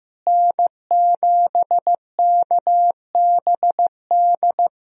Morse code, heard as a keyed tone, about 15 wpm, 700 Hz. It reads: N7KBD
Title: USA Call Signs - 5 Characters - Repeated 3x - 15wpm